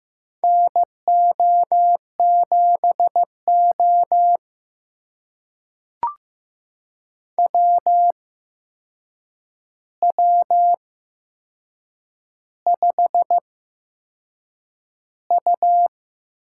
Morse code, heard as a keyed tone, about 15 wpm, spaced as for 6 wpm, 700 Hz. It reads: NO7O WW5U
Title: USA Call Signs - 4 Characters - 8x Character Spacing - 15wpm